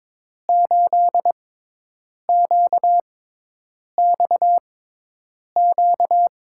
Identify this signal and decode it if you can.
Morse code, heard as a keyed tone, about 22 wpm, spaced as for 8 wpm, 700 Hz. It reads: 8QXQ
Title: USA Call Signs - 6 Characters - 6x Character Spacing - 22wpm